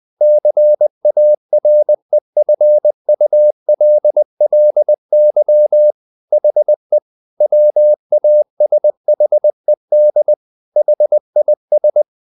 Morse code, heard as a keyed tone, about 20 wpm, 600 Hz. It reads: CAREFULLY HE WASHED HIS
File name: the old man and the sea / the old man and the sea part 1 20wpm